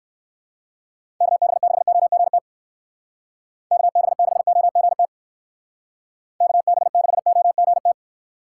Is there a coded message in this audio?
Yes: Morse code from 1.2 to 7.9 s, heard as a keyed tone, about 59 wpm, 700 Hz. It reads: KB6OGT KB6OGT KB6OGT